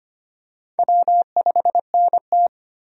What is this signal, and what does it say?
Morse code, heard as a keyed tone, about 25 wpm, 700 Hz. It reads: W5NT
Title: USA Call Signs - 4 Characters - 25wpm